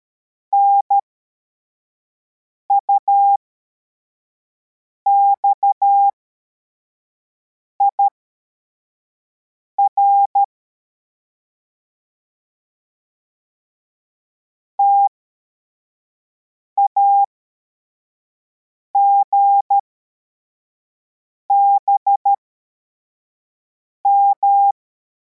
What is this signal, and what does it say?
Morse code, heard as a keyed tone, about 13 wpm, spaced as for 4 wpm, 800 Hz. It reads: NUXIR TAGBM
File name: formation / cw / Text44